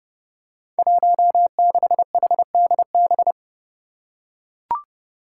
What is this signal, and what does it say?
Morse code, heard as a keyed tone, about 30 wpm, 700 Hz. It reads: 16HDB
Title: Five Challenging Letter-Number Combinations 30wpm